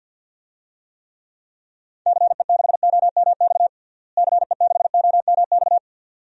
Morse code, heard as a keyed tone, about 51 wpm, 700 Hz. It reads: CE6OMX CE6OMX
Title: DXCC-100 Call Signs - 6 Characters - Repeated 2x - 50wpm